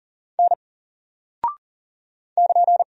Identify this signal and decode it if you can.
Morse code, heard as a keyed tone, about 40 wpm, 700 Hz. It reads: N <KN>
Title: Single Character (Letter:Number:Punctuation:Prosign) 40wpm